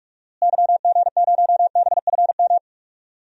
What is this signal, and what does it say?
Morse code, heard as a keyed tone, about 45 wpm, 700 Hz. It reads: YG0BFM
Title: DXCC-100 Call Signs - 6 Characters - Repeated 2x - 45wpm